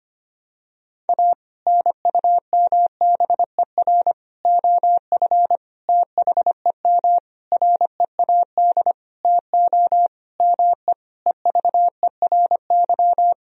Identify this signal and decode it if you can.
Morse code, heard as a keyed tone, about 25 wpm, 700 Hz. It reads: A NUMBER OF THEM READ TO ME EVERY
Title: Sentences from Top 1000 Words - Review Part 1 - 25wpm